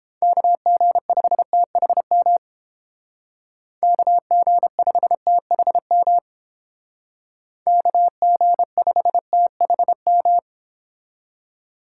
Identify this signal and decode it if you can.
Morse code, heard as a keyed tone, about 29 wpm, 700 Hz. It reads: KG5THM KG5THM KG5THM